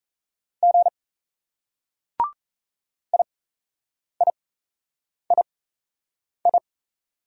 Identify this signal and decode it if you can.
Morse code, heard as a keyed tone, about 38 wpm, spaced as for 8 wpm, 700 Hz. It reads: G IIII